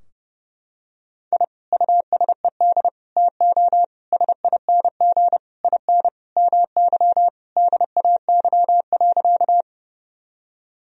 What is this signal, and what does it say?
Morse code, heard as a keyed tone, about 30 wpm, 700 Hz. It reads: I USED TO SING IN MY DAY.
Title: Sentences from Top 1000 Words - Review Part 2 - 30wpm